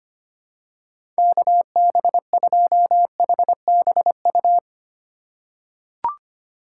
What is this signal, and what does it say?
Morse code, heard as a keyed tone, about 25 wpm, 700 Hz. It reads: KB2HBU
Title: USA Call Signs - 6 Characters - 2x Character Spacing - 25wpm